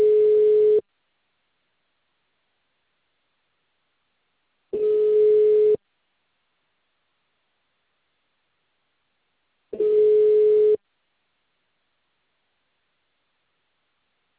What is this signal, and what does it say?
Ringback tone (Europe, 425 Hz).
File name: out- -2030